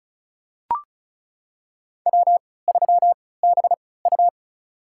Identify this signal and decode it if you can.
Morse code, heard as a keyed tone, about 36 wpm, spaced as for 20 wpm, 700 Hz. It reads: W3BU
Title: USA Call Signs - 4 Characters - 3x Character Spacing - 35wpm